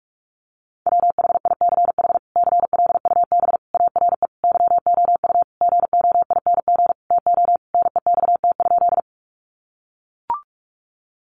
Morse code, heard as a keyed tone, about 44 wpm, 700 Hz. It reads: WHICH CLUB ARE YOU GOING TO NEXT?
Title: Sentences from Top 1200 Words 45wpm